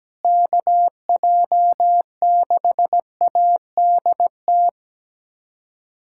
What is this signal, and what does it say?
Morse code, heard as a keyed tone, about 17 wpm, 700 Hz. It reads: KJ6ADT
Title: USA Callsign DE Callsign 17wpm